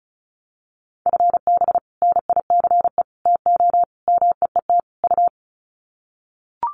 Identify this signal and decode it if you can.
Morse code, heard as a keyed tone, about 34 wpm, 700 Hz. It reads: FB NICE TO MEET U
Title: Common QSO Phrases 35wpm